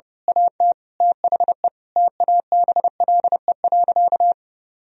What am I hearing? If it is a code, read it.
Morse code, heard as a keyed tone, about 32 wpm, 700 Hz. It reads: AT THE TABLE.
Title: Sentences from Top 1000 Words - Review Part 2 - 30wpm